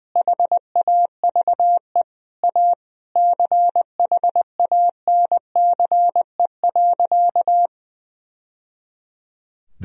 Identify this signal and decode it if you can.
Morse code, heard as a keyed tone, about 20 wpm, 700 Hz. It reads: HAVE A CHANCE.
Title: Sentences from Top 1000 Words - Review Part 2 - 20wpm